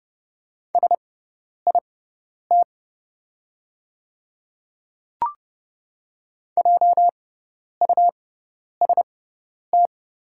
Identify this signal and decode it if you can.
Morse code, heard as a keyed tone, about 30 wpm, spaced as for 10 wpm, 690 Hz. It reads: SIT JUST
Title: 6x Character Spacing - Top 500 Words - 30wpm